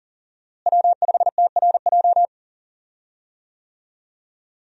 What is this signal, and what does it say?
Morse code, heard as a keyed tone, about 40 wpm, 700 Hz. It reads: W5TRJ
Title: USA Call Signs - 5 Characters - Repeated 3x - 40wpm